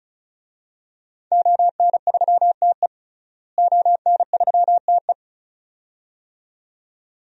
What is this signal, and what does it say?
Morse code, heard as a keyed tone, about 36 wpm, 700 Hz. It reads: ON3TE ON3TE
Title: DXCC-100 Call Signs - 5 Characters - Repeated 2x - 35wpm